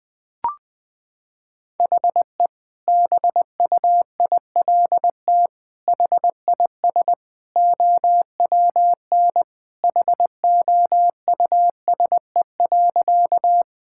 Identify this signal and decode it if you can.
Morse code, heard as a keyed tone, about 20 wpm, 700 Hz. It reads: HE BUILT HIS OWN HOUSE.
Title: Sentences with 2-Word Phrases - 20wpm